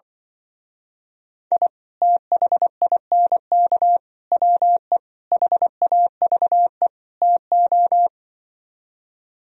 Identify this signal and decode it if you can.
Morse code, heard as a keyed tone, about 24 wpm, 700 Hz. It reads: I THINK WE HAVE TO
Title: Speed-Racing - Sets of 5 Words 20wpm